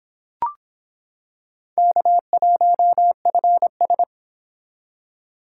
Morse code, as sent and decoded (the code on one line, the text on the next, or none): -.- .---- ..-. ...
K1FS